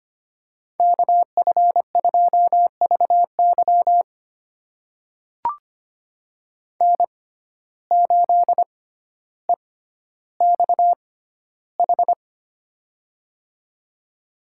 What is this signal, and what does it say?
Morse code, heard as a keyed tone, about 25 wpm, spaced as for 13 wpm, 700 Hz. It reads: KF2VY N8EXH
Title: USA Call Signs - 5 Characters - 6x Character Spacing - 25wpm